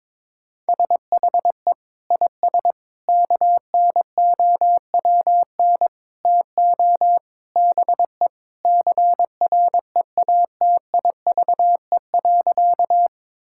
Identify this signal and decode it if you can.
Morse code, heard as a keyed tone, about 22 wpm, 700 Hz. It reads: SHE IS KNOWN TO BE CREATIVE.
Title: Sentences from Top 1000 Words - Review Part 5 - 22wpm